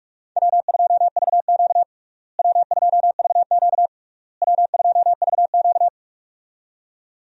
Morse code, heard as a keyed tone, about 41 wpm, 700 Hz. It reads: W2VQ W2VQ W2VQ
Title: USA Call Signs - 4 Characters - Repeated 3x - 45wpm